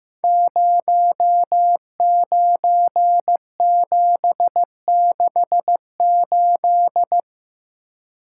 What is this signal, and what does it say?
Morse code, heard as a keyed tone, about 15 wpm, 700 Hz. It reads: 09768
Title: Code Groups - 5 Letter:Numbers - 15wpm